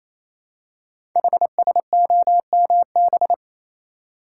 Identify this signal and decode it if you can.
Morse code, heard as a keyed tone, about 27 wpm, 700 Hz. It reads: HSOMB
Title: Code Groups - 5 Letters - 28wpm